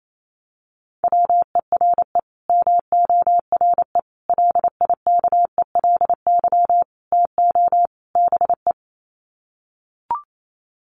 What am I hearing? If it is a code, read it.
Morse code, heard as a keyed tone, about 28 wpm, 700 Hz. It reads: WERE MORE LIKELY TO BE
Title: Sets of 5 Words Encore2 28wpm